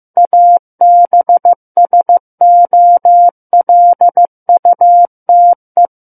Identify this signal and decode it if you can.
Morse code, heard as a keyed tone, about 15 wpm, 700 Hz. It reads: ABSOLUTE
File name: Book One / 06-Chapter 6 - 15 WPM